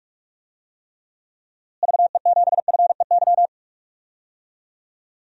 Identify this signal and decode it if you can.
Morse code, heard as a keyed tone, about 43 wpm, 700 Hz. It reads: VE7FEY
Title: DXCC-100 Call Signs - 6 Characters - Repeated 3x - 45wpm